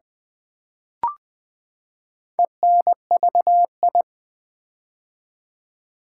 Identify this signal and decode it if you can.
Morse code, heard as a keyed tone, about 20 wpm, 700 Hz. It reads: ENVI
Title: Build A Word - 4 Letters At A Time - 11-Letter Words - 20wpm